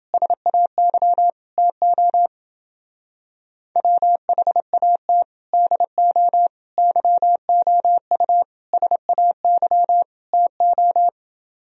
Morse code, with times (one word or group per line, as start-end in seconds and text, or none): 0.1-1.3 SAY
1.6-2.3 TO
3.8-5.2 WHAT
5.5-6.5 DO
6.8-8.4 YOU
8.7-10.0 SAY
10.3-11.1 TO